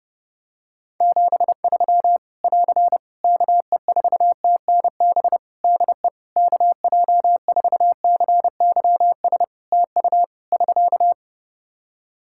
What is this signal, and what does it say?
Morse code, heard as a keyed tone, about 31 wpm, 700 Hz. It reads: 73 <AR> KE4TNB DE KJ4CYS TU <SK>
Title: Practice QSO - Exchange 4 - Ending - 30wpm